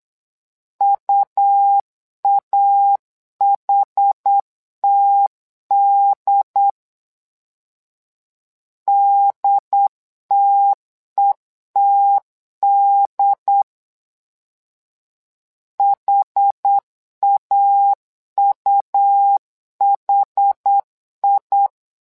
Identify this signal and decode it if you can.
Morse code, heard as a keyed tone, about 8 wpm, 800 Hz. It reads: UAHTD DTETD HAUHI
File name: LEZIONI / 2 UHDN / 2 UHD 8wpm 2258